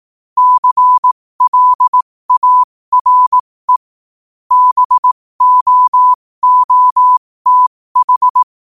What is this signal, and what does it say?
Morse code, heard as a keyed tone, about 18 wpm, spaced as for 15 wpm, 1000 Hz. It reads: CLARE BOOTH
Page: New quotes every day in morse code at 15 Words per minute.